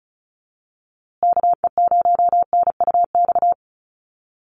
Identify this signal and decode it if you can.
Morse code, heard as a keyed tone, about 34 wpm, 700 Hz. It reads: KE0NUX
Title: USA Call Signs Repeated 3x 35wpm